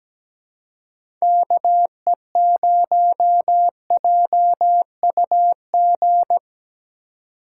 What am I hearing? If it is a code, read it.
Morse code, heard as a keyed tone, about 17 wpm, 700 Hz. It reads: KE0JUG
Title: USA Call Signs - 6 Characters - 2x Character Spacing - 17wpm